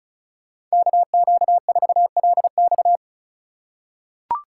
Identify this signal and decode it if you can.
Morse code, heard as a keyed tone, about 36 wpm, 700 Hz. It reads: KQ4LX
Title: USA Call Signs - 5 Characters - 2x Character Spacing - 35wpm